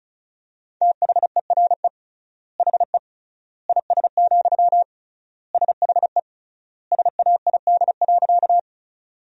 Morse code, heard as a keyed tone, about 36 wpm, 700 Hz. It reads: THERE HE IS, SHE SAID.
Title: 3x Word-Spacing - Sentences from Top 100 Words - 35wpm